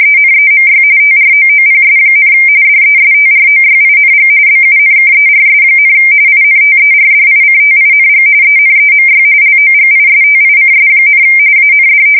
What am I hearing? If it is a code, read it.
RTTY, about 45.45 baud. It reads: ENEMY TROOPS, DESTROYING BUNKERS, INTERDICTING ENEMY LOGISTICS, AND
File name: fsk170 / battleshipnj